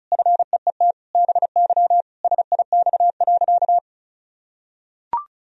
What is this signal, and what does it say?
Morse code, heard as a keyed tone, about 36 wpm, 700 Hz. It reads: FEET BY SIX.
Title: Sentences from Top 1000 Words - Review Part 2 - 35wpm